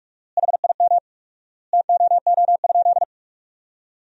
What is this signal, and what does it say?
Morse code, heard as a keyed tone, about 46 wpm, 700 Hz. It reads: HIM TOO?
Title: 4x Word-Spacing - Sentences from Top 500 Words - 45wpm